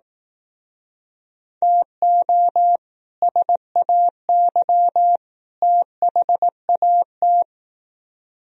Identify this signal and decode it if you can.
Morse code, heard as a keyed tone, about 19 wpm, 700 Hz. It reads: TO SAY THAT
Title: Speed-Racing - Sets of 3 Words 15wpm